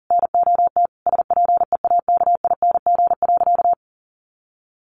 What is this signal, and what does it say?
Morse code, heard as a keyed tone, about 40 wpm, 700 Hz. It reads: NOT SPEAKING.
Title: A Sound Of Thunder 40wpm